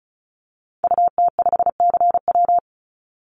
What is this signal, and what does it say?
Morse code, heard as a keyed tone, about 34 wpm, 700 Hz. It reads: UT5CW
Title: Call Signs Repeated 2x - 35wpm